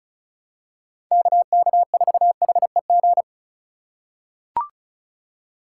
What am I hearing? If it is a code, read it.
Morse code, heard as a keyed tone, about 36 wpm, 700 Hz. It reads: KK4HEG
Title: K1USN SST QSO Practice - 35wpm